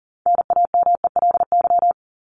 Morse code, heard as a keyed tone, about 40 wpm, 700 Hz. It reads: NAMELY
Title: LY Suffix - Top 100 Words with Base Words 40wpm